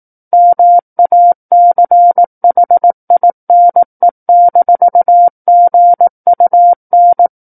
Morse code, heard as a keyed tone, about 18 wpm, 700 Hz. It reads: MACHINE-GUN
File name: Book One / 09-Chapter 9 - 18 WPM